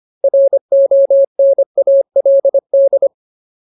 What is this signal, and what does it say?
Morse code, heard as a keyed tone, about 25 wpm, 550 Hz. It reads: RONALD
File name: Ultimate Word List / 25WPM Ultimate